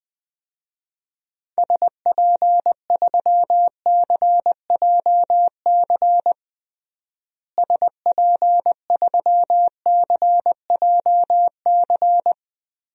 Morse code, heard as a keyed tone, about 20 wpm, 700 Hz. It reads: SP3CJC SP3CJC
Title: DXCC-100 Call Signs - 6 Characters - Repeated 3x - 20wpm